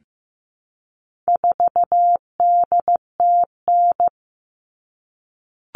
Morse code, heard as a keyed tone, about 15 wpm, 700 Hz. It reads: 4DTN
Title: All Cut Numbers - 4-Digit 15wpm